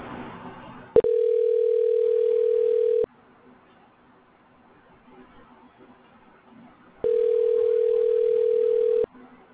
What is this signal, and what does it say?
Ringback tone (North America), 440 + 480 Hz.